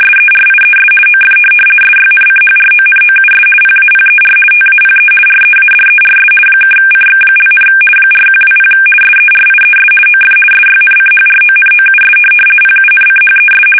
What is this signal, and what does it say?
RTTY, about 50 baud. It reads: SERIES OF SYNCHRONOUS PAPER TAPE READERS (CX), 4120 SERIES OF SYNCHRONOUS PUNCHES
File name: fsk850-50B / teletype-corp